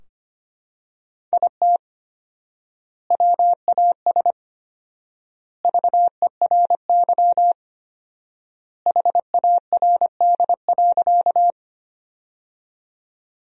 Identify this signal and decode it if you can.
Morse code, heard as a keyed tone, about 25 wpm, 700 Hz. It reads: IT WAS VERY HARD.
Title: 4x Word-Spacing - Sentences from Top 300 Words - 25wpm